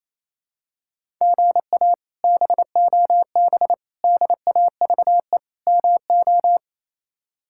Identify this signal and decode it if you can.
Morse code, heard as a keyed tone, about 28 wpm, 700 Hz. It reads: GA BOB DAVE MO
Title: K1USN SST QSO Practice - 28wpm